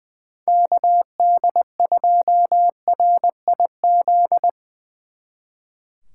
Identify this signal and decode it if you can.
Morse code, heard as a keyed tone, about 22 wpm, 700 Hz. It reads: KD2RIZ
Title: Speed Racing - USA 2x3 Call Signs 20wpm